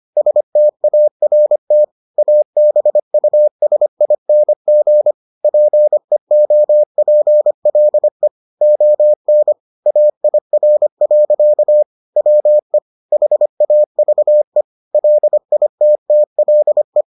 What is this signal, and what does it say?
Morse code, heard as a keyed tone, about 25 wpm, 600 Hz. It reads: START ABUSING PEOPLE ON AIR. WE HAVE LITTLE